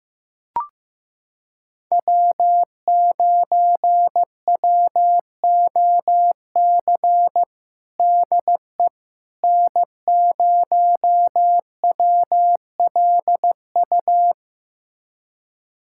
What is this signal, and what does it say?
Morse code, heard as a keyed tone, about 15 wpm, 700 Hz. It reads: W9WOC DE N0WLU
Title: USA Callsign DE Callsign - 5 Characters - 15wpm